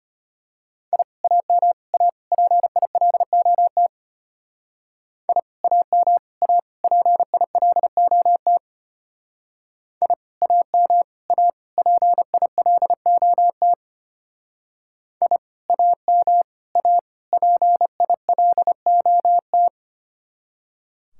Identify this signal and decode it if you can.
Morse code, heard as a keyed tone, about 32 wpm, 700 Hz. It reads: I AM A PILOT I AM A PILOT I AM A PILOT I AM A PILOT